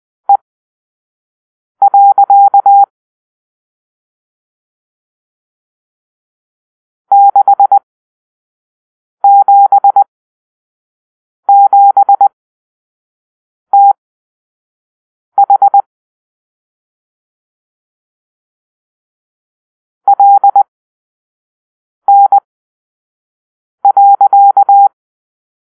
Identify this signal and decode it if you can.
Morse code, heard as a keyed tone, about 20 wpm, spaced as for 5 wpm, 800 Hz. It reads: E. 677TH LN.
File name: arrl / Grp-7 Drill-3